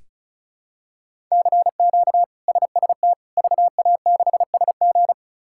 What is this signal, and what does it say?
Morse code, heard as a keyed tone, about 34 wpm, 700 Hz. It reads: CQ SST VA6SG
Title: K1USN SST QSO Practice - 35wpm